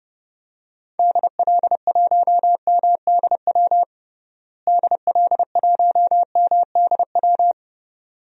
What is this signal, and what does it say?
Morse code, heard as a keyed tone, about 32 wpm, 700 Hz. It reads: DL1MDW DL1MDW
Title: DXCC-100 Call Signs - 6 Characters - Repeated 2x - 30wpm